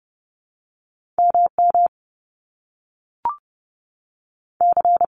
Morse code, heard as a keyed tone, about 30 wpm, 700 Hz. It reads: M M C
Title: Top Two Letter Combinations in Words 30wpm